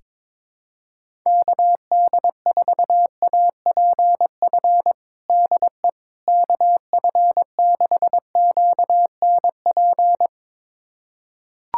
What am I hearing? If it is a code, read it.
Morse code, heard as a keyed tone, about 22 wpm, 700 Hz. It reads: KD4APF DE KF6QNP
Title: USA Callsign DE Callsign - 6 Characters - 22wpm